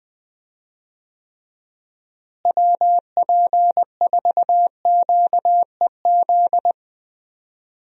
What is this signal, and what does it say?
Morse code, heard as a keyed tone, about 21 wpm, 700 Hz. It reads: WP4QEZ
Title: USA Call Signs - 6 Characters - Repeated 2x - 20wpm